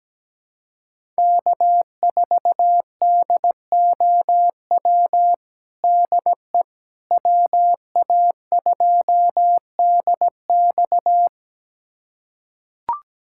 Morse code, heard as a keyed tone, about 17 wpm, 700 Hz. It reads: K4DOW DE WA2DX
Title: USA Callsign DE Callsign - 5 Characters - 17wpm